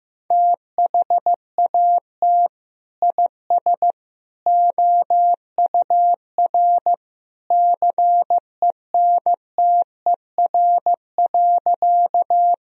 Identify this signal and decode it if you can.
Morse code, heard as a keyed tone, about 15 wpm, 700 Hz. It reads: THAT IS OUR CENTER.
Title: Sentences from Top 1000 Words - Review Part 2 - 15wpm